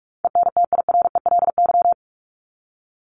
Morse code, heard as a keyed tone, about 44 wpm, 700 Hz. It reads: ENTIRELY
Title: LY Suffix - Top 100 Words with Base Words 45wpm